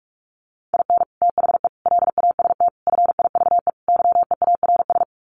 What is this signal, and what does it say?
Morse code, heard as a keyed tone, about 44 wpm, 700 Hz. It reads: IN THE LAST FIVE YEARS
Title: Sets of 5 Words Encore 45wpm